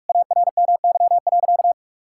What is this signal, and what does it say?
Morse code, heard as a keyed tone, about 46 wpm, 700 Hz. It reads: ARMY.
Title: Sentences from Top 1000 Words - Review Part 6 - 45wpm